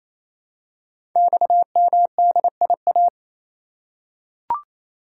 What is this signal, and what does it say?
Morse code, heard as a keyed tone, about 29 wpm, 690 Hz. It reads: XMDIA
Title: Code Groups - 5 Letters - 28wpm